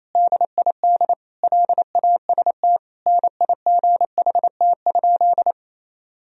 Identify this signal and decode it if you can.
Morse code, heard as a keyed tone, about 28 wpm, 700 Hz. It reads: DID LAST NIGHT?
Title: Sentences from Top 300 Words and Abbreviations 28wpm